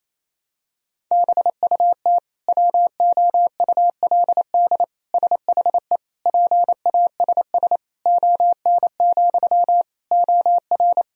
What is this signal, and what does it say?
Morse code, heard as a keyed tone, about 28 wpm, 700 Hz. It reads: BUT WOULD SHE PASS ON, OR
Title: Sentences from Top 1000 Words - Review Part 1 - 28wpm